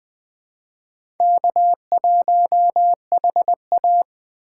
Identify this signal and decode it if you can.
Morse code, heard as a keyed tone, about 20 wpm, 700 Hz. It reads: K1HA
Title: USA Call Signs - 4 Characters - 4x Character Spacing - 20wpm